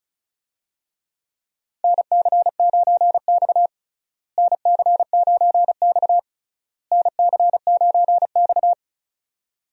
Morse code, heard as a keyed tone, about 36 wpm, 700 Hz. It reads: NC9X NC9X NC9X
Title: USA Call Signs - 4 Characters - Repeated 3x - 35wpm